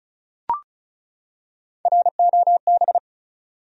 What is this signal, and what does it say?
Morse code, heard as a keyed tone, about 36 wpm, 700 Hz. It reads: ROB
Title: USA Common Names - 35wpm